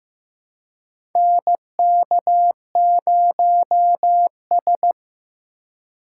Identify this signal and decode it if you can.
Morse code, heard as a keyed tone, about 15 wpm, 700 Hz. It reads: NK0S
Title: USA Call Signs - 4 Characters - 4x Character Spacing - 15wpm